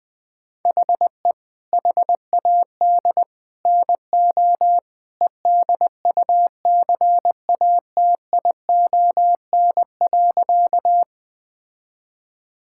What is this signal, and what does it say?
Morse code, heard as a keyed tone, about 20 wpm, 700 Hz. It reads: HE HAD NO EDUCATION.
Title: Sentences from Top 1000 Words - Review Part 6 - 20wpm